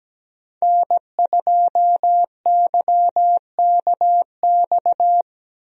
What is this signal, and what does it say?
Morse code, heard as a keyed tone, about 17 wpm, 700 Hz. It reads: N2YKX
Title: USA Call Signs - 5 Characters - Repeated 2x - 17wpm